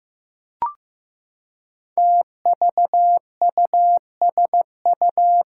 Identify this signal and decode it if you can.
Morse code, heard as a keyed tone, about 15 wpm, 700 Hz. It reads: TVUSU
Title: Five Challenging Letter-Number Combinations 15wpm